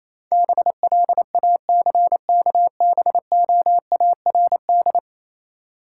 Morse code, heard as a keyed tone, about 27 wpm, 690 Hz. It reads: BLACKBOARD
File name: Easy Words - Speed-Racing with Hints - 22wpm